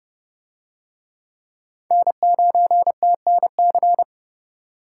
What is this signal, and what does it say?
Morse code, heard as a keyed tone, about 28 wpm, 700 Hz. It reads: N9TNC